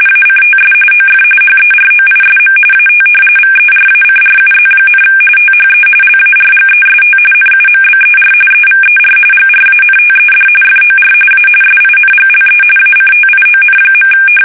RTTY, about 45.45 baud. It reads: ACTS OF HEROISM AT VERACRUZ, THE LARGEST NUMBER EVER GRANTED FOR A SINGLE ACTION.
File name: fsk850